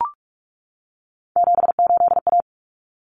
Morse code, heard as a keyed tone, about 44 wpm, 700 Hz. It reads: 78A
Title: Common Cut Numbers - 3-Digit 45wpm